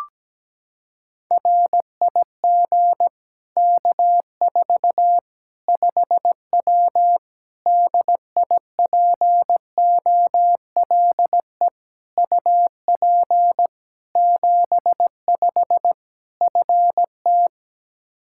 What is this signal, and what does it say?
Morse code, heard as a keyed tone, about 17 wpm, 700 Hz. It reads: RIG K4 5W DIPOLE UP 75 FT
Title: Practice QSO - Exchange 2 - Info - 17wpm